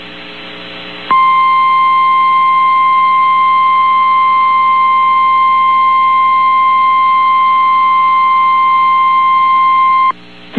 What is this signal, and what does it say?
Emergency Alert System (SAME) tones (the 1050 Hz attention tone). It is heard from 1.0 to 10.2 s.
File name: eas received alerts / L1 19461 audio